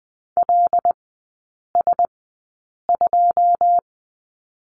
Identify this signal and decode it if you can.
Morse code, heard as a keyed tone, about 20 wpm, spaced as for 8 wpm, 690 Hz. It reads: LS2
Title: 133 Lesson 22e - Question Mark - Three Characters 20wpm